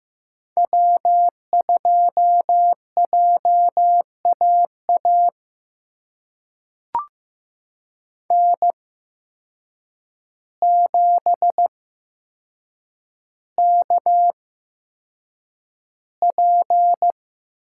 Morse code, heard as a keyed tone, about 15 wpm, spaced as for 7 wpm, 700 Hz. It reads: W2JAA N7KP
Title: USA Call Signs - 8x Character Spacing - 15wpm